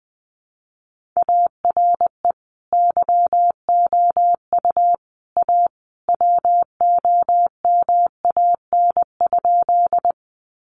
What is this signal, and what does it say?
Morse code, heard as a keyed tone, about 20 wpm, 700 Hz. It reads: ARE YOU A WOMAN?
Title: Sentences from Top 1100 Words 20wpm